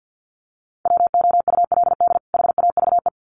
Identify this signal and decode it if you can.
Morse code, heard as a keyed tone, about 50 wpm, 700 Hz. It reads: WOULD HAVE
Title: Sets of 2 Words 50wpm